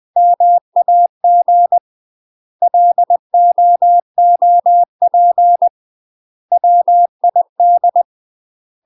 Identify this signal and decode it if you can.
Morse code, heard as a keyed tone, about 20 wpm, 700 Hz. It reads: MAG LOOP WID
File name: UK MASTER QSOs TRAINING / 20WPM / UK MASTER QSOs 0010